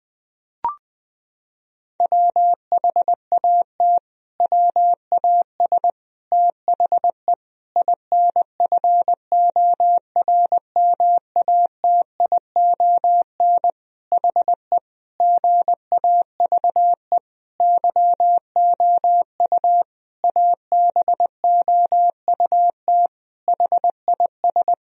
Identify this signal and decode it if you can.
Morse code, heard as a keyed tone, about 20 wpm, 700 Hz. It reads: WHAT WAS THE INFORMATION HE GAVE YOU ABOUT HIS
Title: Sentences from Top 1000 Words - Review Part 3 - 20wpm